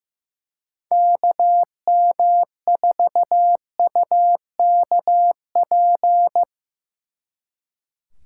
Morse code, heard as a keyed tone, about 17 wpm, 700 Hz. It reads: KM4UKP
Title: Speed Racing - USA 2x3 Call Signs 15wpm